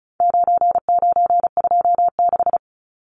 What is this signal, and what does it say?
Morse code, heard as a keyed tone, about 34 wpm, 700 Hz. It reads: 9926
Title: Numbers - 4-Digit 35wpm